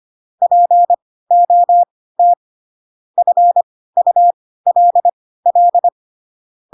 Morse code, heard as a keyed tone, about 25 wpm, spaced as for 16 wpm, 700 Hz. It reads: POT FULL
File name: Pinocchio / Pinocchio ch3 10 25 ik1hge